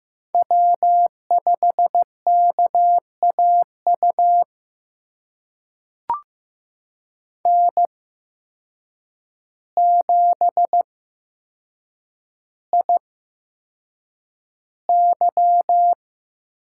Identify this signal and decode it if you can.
Morse code, heard as a keyed tone, about 15 wpm, spaced as for 7 wpm, 700 Hz. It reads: W5KAU N7IY
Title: USA Call Signs - 8x Character Spacing - 15wpm